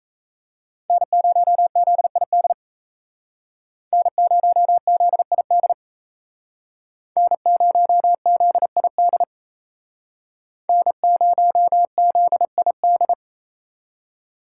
Morse code, heard as a keyed tone, about 35 wpm, 700 Hz. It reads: N0ZID N0ZID N0ZID N0ZID
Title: Speed Racing - USA 1x3 Call Signs 28wpm